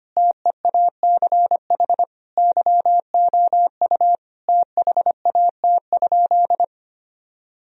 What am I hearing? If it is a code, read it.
Morse code, heard as a keyed tone, about 25 wpm, 700 Hz. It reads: TEACH YOU THAT?
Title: Sentences from Top 1000 Words - Review Part 2 - 25wpm